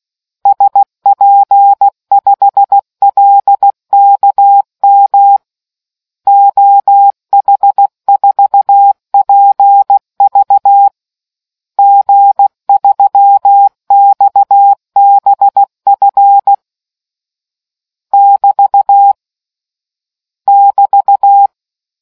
Morse code, heard as a keyed tone, about 16 wpm, 800 Hz. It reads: SP5LKM OH4PV G3XBF <BT> <BT>